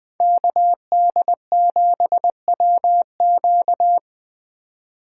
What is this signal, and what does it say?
Morse code, heard as a keyed tone, about 20 wpm, 700 Hz. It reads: KD7WQ
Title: Speed Racing - USA 2x2 Call Signs 17wpm